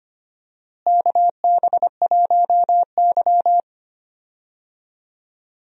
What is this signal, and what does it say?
Morse code, heard as a keyed tone, about 25 wpm, 700 Hz. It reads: KB1Y
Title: USA Call Signs - 4 Characters - Repeated 3x - 25wpm